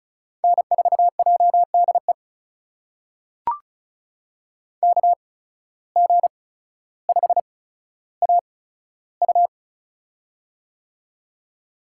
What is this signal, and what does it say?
Morse code, heard as a keyed tone, about 36 wpm, spaced as for 15 wpm, 700 Hz. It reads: N4JDE KG5AU
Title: USA Call Signs - 5 Characters - 8x Character Spacing - 35wpm